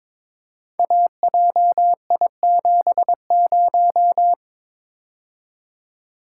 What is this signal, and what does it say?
Morse code, heard as a keyed tone, about 22 wpm, 700 Hz. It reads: AJI70
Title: Code Groups - 5 Letter:Numbers - 22wpm